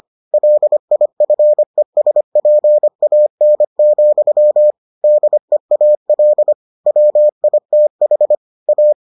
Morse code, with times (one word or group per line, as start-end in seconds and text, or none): 0.3-4.7 LIFESPAN,
5.0-6.5 DEAL
6.9-8.4 WITH
8.7-8.9 A